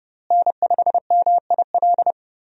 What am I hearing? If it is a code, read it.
Morse code, heard as a keyed tone, about 30 wpm, 700 Hz. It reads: N5MIL
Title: USA Call Signs - 5 Characters - Repeated 3x - 30wpm